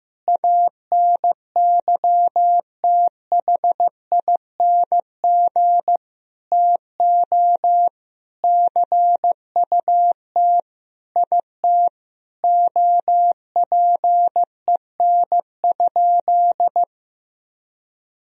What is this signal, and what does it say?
Morse code, heard as a keyed tone, about 15 wpm, 700 Hz. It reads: ANYTHING TO CUT IT OPEN?
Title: Sentences from Top 1000 Words - Review Part 1 - 15wpm